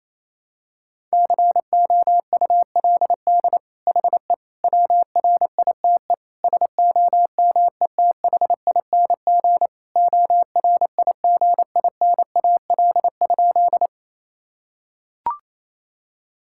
Morse code, heard as a keyed tone, about 28 wpm, 700 Hz. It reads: COULD HE WRITE SOMETHING ORIGINAL?
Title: Sentences from Top 1000 Words - Review Part 6 - 28wpm